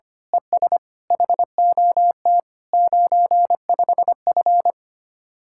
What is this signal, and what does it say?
Morse code, heard as a keyed tone, about 24 wpm, 700 Hz. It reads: ES HOT 95F
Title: Speed-Racing Common QSO Phrases - 25wpm